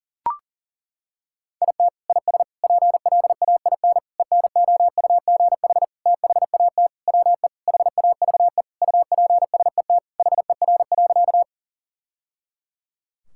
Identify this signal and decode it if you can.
Morse code, heard as a keyed tone, about 37 wpm, 700 Hz. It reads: IT IS PLAIN ENOUGH THAT WE HAVE UPSET HER.
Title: Sentences from Top 1000 Words - Review Part 5 - 40wpm